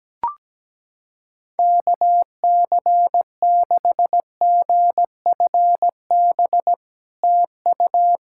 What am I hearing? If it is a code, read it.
Morse code, heard as a keyed tone, about 17 wpm, 700 Hz. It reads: KC6GFB TU
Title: SOTA QSO Practice 17wpm